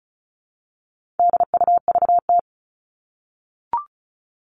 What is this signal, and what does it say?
Morse code, heard as a keyed tone, about 34 wpm, 700 Hz. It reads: DUVT
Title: All Cut Numbers - 4-Digit 35wpm